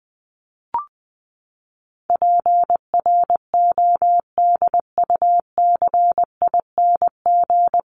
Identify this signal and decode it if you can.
Morse code, heard as a keyed tone, about 20 wpm, 700 Hz. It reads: PRODUCING
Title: ING Suffix - Top 100 Words with Base Words 20wpm